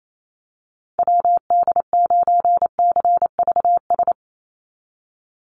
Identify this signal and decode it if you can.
Morse code, heard as a keyed tone, about 28 wpm, 700 Hz. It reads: WD9CVS
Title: USA Call Signs 28wpm